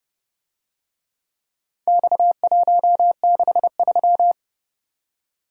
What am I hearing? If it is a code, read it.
Morse code, heard as a keyed tone, about 30 wpm, 700 Hz. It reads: X163